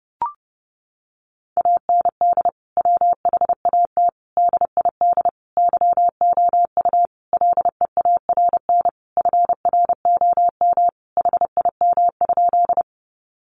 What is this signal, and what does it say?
Morse code, heard as a keyed tone, about 28 wpm, 700 Hz. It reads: AND WHAT DID YOU LEARN FROM HIM?
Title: Sentences from Top 300 Words 30wpm